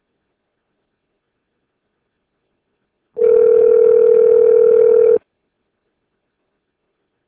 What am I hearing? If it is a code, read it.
Ringback tone (North America), 440 + 480 Hz.